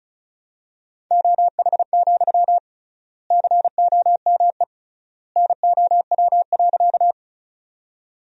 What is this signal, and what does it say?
Morse code, heard as a keyed tone, about 34 wpm, 700 Hz. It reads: OH, COME NOW.
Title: 3x Word-Spacing - Sentences from Top 500 Words - 35wpm